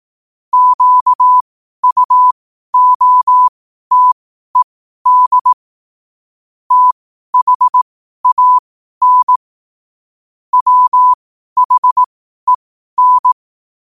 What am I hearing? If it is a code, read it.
Morse code, heard as a keyed tone, about 18 wpm, spaced as for 12 wpm, 1000 Hz. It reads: QUOTED THAN WHEN
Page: Quotes for Wed, 13 Aug 2025 in Morse Code at 12 words per minute.